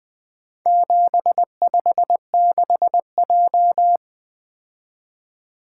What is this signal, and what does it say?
Morse code, heard as a keyed tone, about 20 wpm, 700 Hz. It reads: 756J
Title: Four Letter-Number Combinations - 20wpm